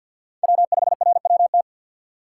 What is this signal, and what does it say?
Morse code, heard as a keyed tone, about 53 wpm, 700 Hz. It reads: W5RWT
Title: USA Call Signs - 5 Characters - 6x Character Spacing - 50wpm